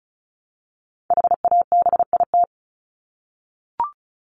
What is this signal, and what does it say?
Morse code, heard as a keyed tone, about 34 wpm, 700 Hz. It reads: HABIT
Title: Top 1751-2000 Words 35wpm